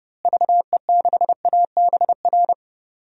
Morse code, heard as a keyed tone, about 30 wpm, 700 Hz. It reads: VE6ABR
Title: DXCC-100 Call Signs - 6 Characters - Repeated 2x - 30wpm